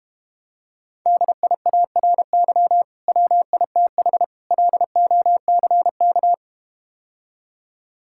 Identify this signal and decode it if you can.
Morse code, heard as a keyed tone, about 32 wpm, 700 Hz. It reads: DIARY WITH LOCK